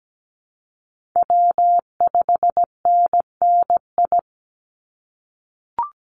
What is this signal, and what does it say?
Morse code, heard as a keyed tone, about 17 wpm, 700 Hz. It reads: W5NNI
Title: USA Call Signs 17wpm